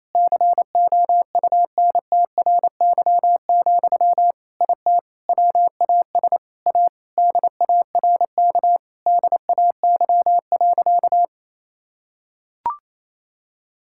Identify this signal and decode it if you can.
Morse code, heard as a keyed tone, about 28 wpm, 700 Hz. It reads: COUNTRY, IT WAS A DARK DAY.
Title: Sentences from Top 1000 Words - Review Part 5 - 28wpm